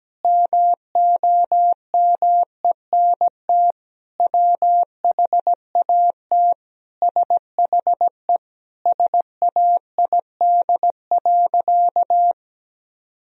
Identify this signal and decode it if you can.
Morse code, heard as a keyed tone, about 17 wpm, 700 Hz. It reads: MOMENT WHAT SHE SAID.
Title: Sentences from Top 1000 Words - Review Part 4 - 17wpm